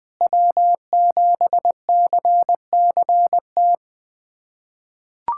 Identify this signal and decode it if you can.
Morse code, heard as a keyed tone, about 21 wpm, 700 Hz. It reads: W7CCT
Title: USA Callsign DE Callsign - 5 Characters - 20wpm